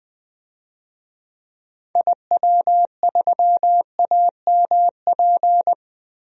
Morse code, heard as a keyed tone, about 21 wpm, 700 Hz. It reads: IW3AMP
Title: DXCC-100 Call Signs - 6 Characters - Repeated 3x - 20wpm